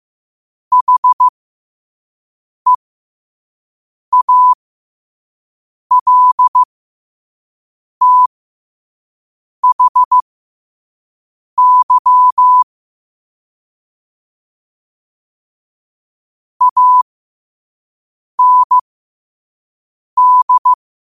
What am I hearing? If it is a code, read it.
Morse code, heard as a keyed tone, about 15 wpm, spaced as for 5 wpm, 1000 Hz. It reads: HEALTHY AND
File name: qotdfiles / 5wpm